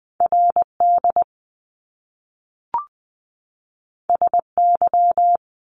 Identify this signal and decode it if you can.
Morse code, heard as a keyed tone, about 20 wpm, 700 Hz. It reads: RD SY
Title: Top Two Letter Combinations in Words 20wpm